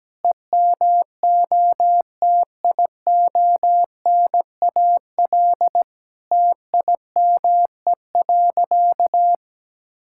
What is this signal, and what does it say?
Morse code, heard as a keyed tone, about 17 wpm, 700 Hz. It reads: EMOTIONAL TIME.
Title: Sentences from Top 1000 Words - Review Part 5 - 17wpm